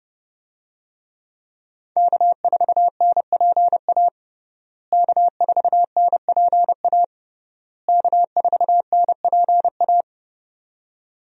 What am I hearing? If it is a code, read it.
Morse code, heard as a keyed tone, about 32 wpm, 700 Hz. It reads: K4NPA K4NPA K4NPA
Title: USA Call Signs - 5 Characters - Repeated 3x - 30wpm